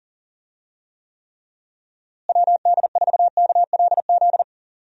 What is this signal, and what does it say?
Morse code, heard as a keyed tone, about 40 wpm, 700 Hz. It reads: WD4KLZ
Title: USA Call Signs - 6 Characters - Repeated 3x - 40wpm